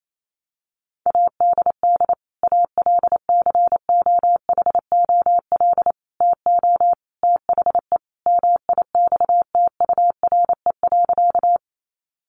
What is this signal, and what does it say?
Morse code, heard as a keyed tone, about 28 wpm, 700 Hz. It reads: ADD ALCOHOL TO THE MIXTURE.
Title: Sentences from Top 1500 Words 28wpm